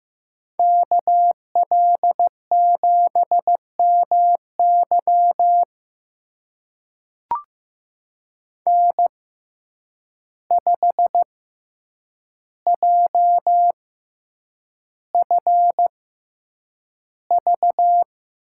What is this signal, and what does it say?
Morse code, heard as a keyed tone, about 15 wpm, spaced as for 8 wpm, 700 Hz. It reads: KL7MY N5JFV
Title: USA Call Signs - 5 Characters - 6x Character Spacing - 15wpm